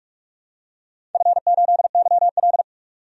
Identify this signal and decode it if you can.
Morse code, heard as a keyed tone, about 46 wpm, 700 Hz. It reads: F8YL
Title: DXCC-100 Call Signs - 4 Characters - 45wpm